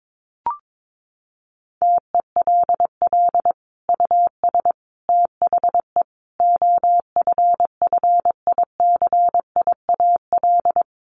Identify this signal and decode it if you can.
Morse code, heard as a keyed tone, about 22 wpm, 700 Hz. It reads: TELL US THE OFFICIAL
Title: Sentences from Top 1400 Words 22wpm